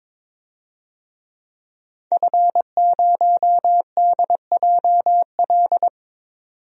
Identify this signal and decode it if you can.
Morse code, heard as a keyed tone, about 22 wpm, 700 Hz. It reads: F0DJL
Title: DXCC-100 Call Signs - 5 Characters - 22wpm